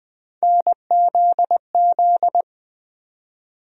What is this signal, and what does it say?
Morse code, heard as a keyed tone, about 20 wpm, 700 Hz. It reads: NZZ
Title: USA Call Sign Postfixes - 3 Letters 20wpm